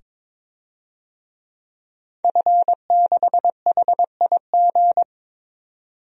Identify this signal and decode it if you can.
Morse code, heard as a keyed tone, about 22 wpm, 700 Hz. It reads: F6HIG
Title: DXCC-100 Call Signs - 5 Characters - 22wpm